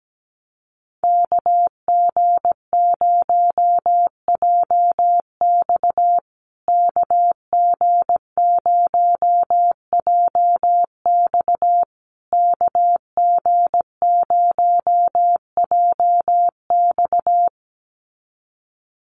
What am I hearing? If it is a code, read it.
Morse code, heard as a keyed tone, about 17 wpm, 700 Hz. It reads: KG0JX KG0JX KG0JX
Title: USA Call Signs Repeated 3x 17wpm